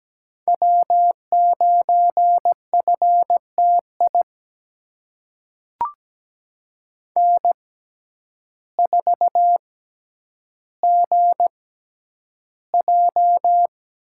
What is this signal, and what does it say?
Morse code, heard as a keyed tone, about 17 wpm, spaced as for 9 wpm, 700 Hz. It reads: W9FTI N4GJ
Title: USA Call Signs - 5 Characters - 6x Character Spacing - 17wpm